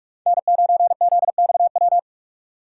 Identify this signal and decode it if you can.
Morse code, heard as a keyed tone, about 46 wpm, 700 Hz. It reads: N9ZXW
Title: USA Call Signs - 5 Characters - 3x Character Spacing - 45wpm